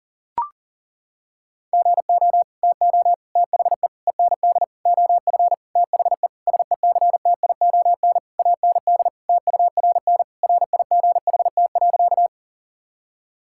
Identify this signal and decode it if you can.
Morse code, heard as a keyed tone, about 40 wpm, 700 Hz. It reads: GO TO THE END OF THE SECTION AND TURN RIGHT.
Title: Sentences from Top 1000 Words - Review Part 5 - 40wpm